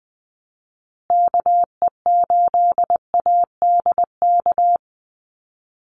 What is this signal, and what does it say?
Morse code, heard as a keyed tone, about 20 wpm, 700 Hz. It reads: KE8ADK
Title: USA Call Signs Repeated 2x 20wpm